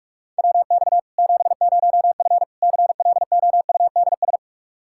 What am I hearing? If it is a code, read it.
Morse code, heard as a keyed tone, about 41 wpm, 700 Hz. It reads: WX 70F CLOUDS
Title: K1USN SST QSO Practice 45wpm